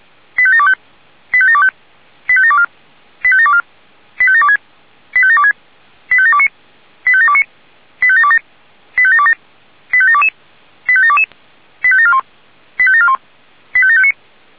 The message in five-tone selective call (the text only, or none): 85725 85725 85724 85724 85726 85726 85729 85729 85727 85727 85722 85722 85741 85741 85759